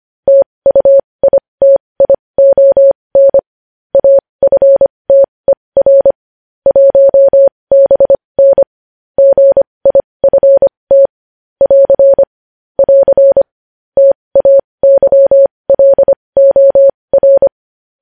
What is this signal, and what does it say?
Morse code, heard as a keyed tone, about 25 wpm, spaced as for 20 wpm, 550 Hz. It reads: TUITION AFTER 1BN GIFT <AR> <AR> TAYLOR